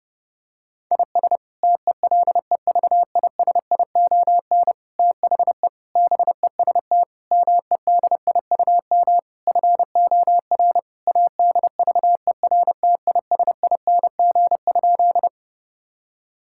Morse code, heard as a keyed tone, about 32 wpm, 700 Hz. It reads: IS TELEVISION THE BEST MEDIUM FOR ADVERTISING?
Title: Sentences from Top 1000 Words - Review Part 6 - 30wpm